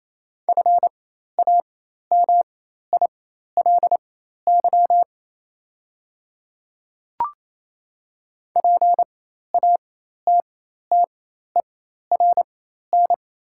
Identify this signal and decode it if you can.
Morse code, heard as a keyed tone, about 28 wpm, spaced as for 13 wpm, 700 Hz. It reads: FAMILY PATTERN
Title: 4x Character Spacing - Top 500 Words - 28wpm